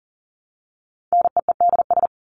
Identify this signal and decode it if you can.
Morse code, heard as a keyed tone, about 40 wpm, 700 Hz. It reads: NEEDS
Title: ES:S Suffix - Top 100 Words with Base Words 40wpm